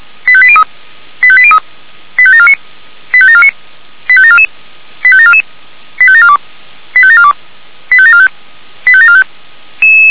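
Five-tone selective call (ZVEI), heard as quick runs of five tones, tones of about 70 ms. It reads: 85803 85803 85749 85749 85744 85744 85742 85742 85745 85745
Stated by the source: ZVEI five-tone selective call received by SDR.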